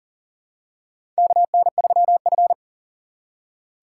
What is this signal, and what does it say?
Morse code, heard as a keyed tone, about 40 wpm, 700 Hz. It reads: KN3F
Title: USA Call Signs - 4 Characters - 2x Character Spacing - 40wpm